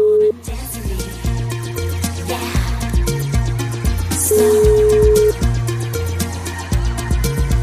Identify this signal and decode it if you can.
Ringback tone (Europe, 425 Hz).